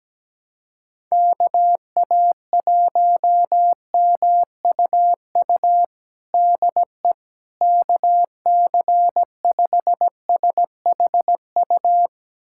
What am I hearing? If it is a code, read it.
Morse code, heard as a keyed tone, about 17 wpm, 700 Hz. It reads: KA1MUU DE KC5SHU
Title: USA Callsign DE Callsign - 6 Characters - 17wpm